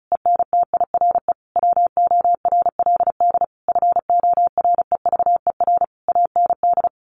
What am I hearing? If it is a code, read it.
Morse code, heard as a keyed tone, about 34 wpm, 700 Hz. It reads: ENTIRE WORLD FOREVER AND
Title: A Sound Of Thunder 35wpm